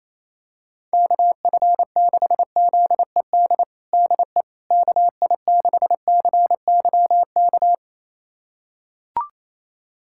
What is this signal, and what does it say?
Morse code, heard as a keyed tone, about 28 wpm, 700 Hz. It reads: KF6ZED DE KI6CYK
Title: USA Callsign DE Callsign - 6 Characters - 28wpm